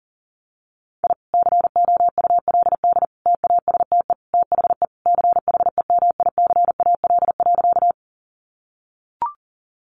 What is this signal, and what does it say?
Morse code, heard as a keyed tone, about 40 wpm, 700 Hz. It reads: I COULD TASTE THE CHEMICAL.
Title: Sentences from Top 1200 Words 40wpm